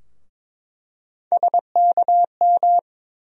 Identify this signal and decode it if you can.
Morse code, heard as a keyed tone, about 22 wpm, 700 Hz. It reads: SKM
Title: USA Call Sign Postfixes - 3 Letters 22wpm